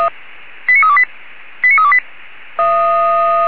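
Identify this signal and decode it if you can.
Five-tone selective call (ZVEI), heard as quick runs of five tones, tones of about 70 ms. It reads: 89328 89328
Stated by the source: ZVEI five-tone call received by SDR, Feuerwehr Orlamünde.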